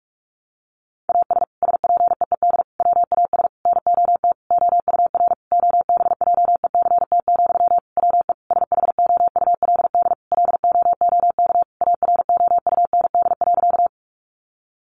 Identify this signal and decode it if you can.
Morse code, heard as a keyed tone, about 44 wpm, 700 Hz. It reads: AS SPEED WAS NOT OUR OBJECT, WE SHOULD LOOK AROUND.
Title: Sentences from Top 800 Words 45wpm